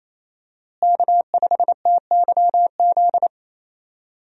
Morse code, heard as a keyed tone, about 27 wpm, 700 Hz. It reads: K5TYZ
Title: USA Call Signs - 5 Characters - Repeated 3x - 28wpm